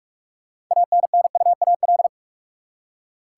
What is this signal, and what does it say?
Morse code, heard as a keyed tone, about 46 wpm, 700 Hz. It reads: ANNUAL
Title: Six Letter Words - 45wpm